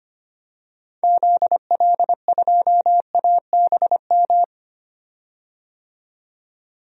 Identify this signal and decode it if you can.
Morse code, heard as a keyed tone, about 25 wpm, 700 Hz. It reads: ZL2ABM
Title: DXCC-100 Call Signs - 6 Characters - 25wpm